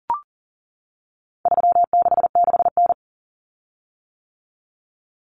Morse code, heard as a keyed tone, about 38 wpm, 700 Hz. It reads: 366N
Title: Common Cut Numbers - 4-Digit 40wpm